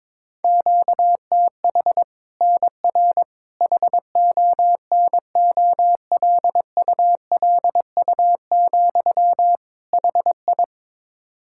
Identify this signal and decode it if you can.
Morse code, heard as a keyed tone, about 22 wpm, 700 Hz. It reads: QTH NR HONOLULU, HI
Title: Practice QSO - Exchange 1 - Beginning - 22wpm